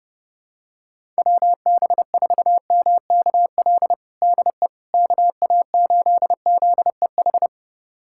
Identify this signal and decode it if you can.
Morse code, heard as a keyed tone, about 30 wpm, 700 Hz. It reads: WB4MKL DE KA8ZEH
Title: USA Callsign DE Callsign - 6 Characters - 30wpm